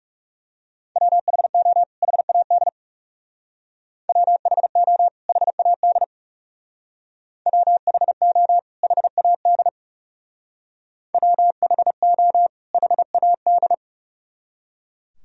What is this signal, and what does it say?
Morse code, heard as a keyed tone, about 39 wpm, 700 Hz. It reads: WHO HAD WHO HAD WHO HAD WHO HAD